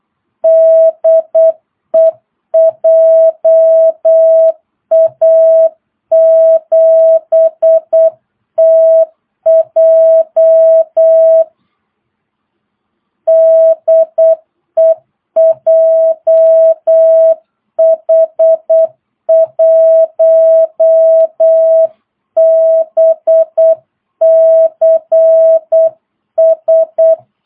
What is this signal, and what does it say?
Morse code, heard as a keyed tone, about 8 wpm, 660 Hz. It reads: DE JA7TJ DE JH1BCS
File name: jh8cbh / 2022cw